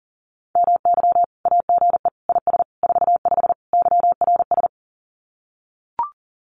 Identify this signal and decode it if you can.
Morse code, heard as a keyed tone, about 40 wpm, 700 Hz. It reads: MY AGE IS 45 YRS
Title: Common QSO Phrases Encore 40wpm